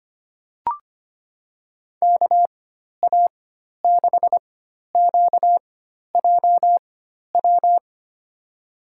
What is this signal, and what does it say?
Morse code, heard as a keyed tone, about 25 wpm, spaced as for 12 wpm, 700 Hz. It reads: KA6QJW
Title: USA Call Signs - 6 Characters - 4x Character Spacing - 25wpm